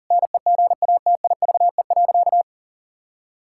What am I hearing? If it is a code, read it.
Morse code, heard as a keyed tone, about 40 wpm, 700 Hz. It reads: NEGATIVE.
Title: Sentences from Top 1000 Words - Review Part 6 - 40wpm